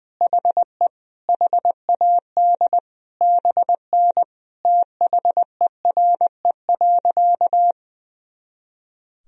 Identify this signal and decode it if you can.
Morse code, heard as a keyed tone, about 20 wpm, 700 Hz. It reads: HE HAD BN THERE.
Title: Sentences from Top 300 Words and Abbreviations 20wpm